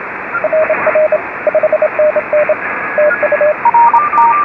Morse code, heard as a keyed tone, about 26 wpm, 620 Hz, with radio noise. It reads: RR 5NN TU
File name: RS44 JA9KRO